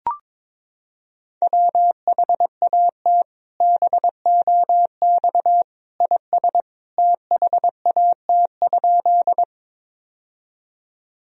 Morse code, heard as a keyed tone, about 21 wpm, 700 Hz. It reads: WHAT BOX IS THAT?
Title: Sentences from Top 1000 Words - Review Part 2 - 22wpm